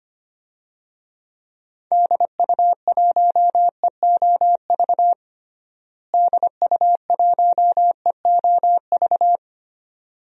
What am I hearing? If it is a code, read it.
Morse code, heard as a keyed tone, about 25 wpm, 700 Hz. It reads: DU1EOV DU1EOV
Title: DXCC-100 Call Signs - 6 Characters - Repeated 2x - 25wpm